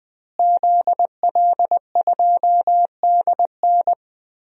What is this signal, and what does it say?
Morse code, heard as a keyed tone, about 21 wpm, 700 Hz. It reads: ZL2DN
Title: DXCC-100 Call Signs - 5 Characters - Repeated 2x - 20wpm